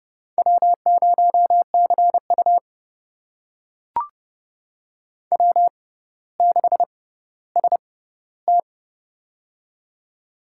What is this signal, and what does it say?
Morse code, heard as a keyed tone, about 30 wpm, spaced as for 15 wpm, 700 Hz. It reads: W0CU W6ST
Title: USA Call Signs - 4 Characters - 6x Character Spacing - 30wpm